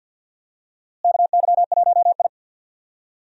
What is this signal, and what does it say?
Morse code, heard as a keyed tone, about 48 wpm, 690 Hz. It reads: KY1I
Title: USA Call Signs - 4 Characters - Repeated 2x - 50wpm